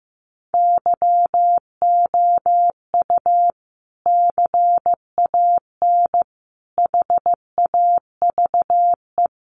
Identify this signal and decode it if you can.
Morse code, heard as a keyed tone, about 15 wpm, 700 Hz. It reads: YOU CAN HAVE
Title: Sentences from Top 100 Words 15wpm